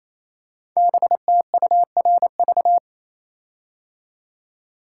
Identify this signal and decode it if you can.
Morse code, heard as a keyed tone, about 27 wpm, 700 Hz. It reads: BTURV
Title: Code Groups - 5 Letters - 28wpm